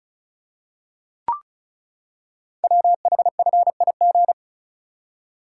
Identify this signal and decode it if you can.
Morse code, heard as a keyed tone, about 36 wpm, 700 Hz. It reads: WHFIG
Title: Code Groups - 5 Letters - 35wpm